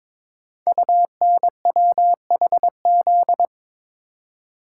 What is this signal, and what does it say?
Morse code, heard as a keyed tone, about 22 wpm, 700 Hz. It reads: UNWHZ
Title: Code Groups - 5 Letters - 22wpm